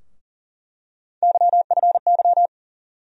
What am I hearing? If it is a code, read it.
Morse code, heard as a keyed tone, about 40 wpm, 700 Hz. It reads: YFY
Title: USA Call Sign Postfixes - 3 Letters 40wpm